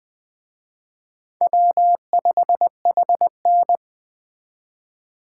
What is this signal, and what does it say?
Morse code, heard as a keyed tone, about 21 wpm, 700 Hz. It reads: W5HN
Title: Speed Racing - USA 1x2 Call Signs 17wpm